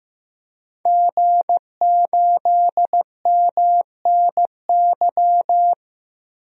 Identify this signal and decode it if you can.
Morse code, heard as a keyed tone, about 15 wpm, 700 Hz. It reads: G8MNY
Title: DXCC-100 Call Signs - 5 Characters - 15wpm